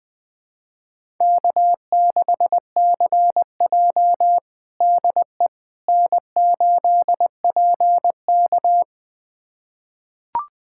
Morse code, heard as a keyed tone, about 20 wpm, 700 Hz. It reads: K6CJ DE N8PK
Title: USA Callsign DE Callsign - 4 Characters - 20wpm